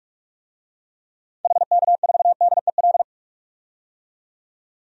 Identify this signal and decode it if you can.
Morse code, heard as a keyed tone, about 46 wpm, 700 Hz. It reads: HK4DEL
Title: DXCC-100 Call Signs - 6 Characters - 45wpm